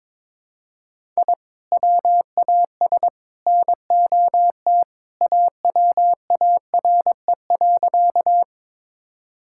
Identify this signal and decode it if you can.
Morse code, heard as a keyed tone, about 22 wpm, 700 Hz. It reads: I WAS NOT AWARE.
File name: practice / 22wpm / Sentences with 3-Word Phrases - 22wpm